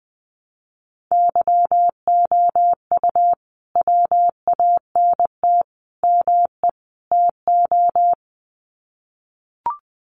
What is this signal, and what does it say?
Morse code, heard as a keyed tone, about 20 wpm, 700 Hz. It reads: YOU WANT ME TO
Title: Sets of 4 Words 20wpm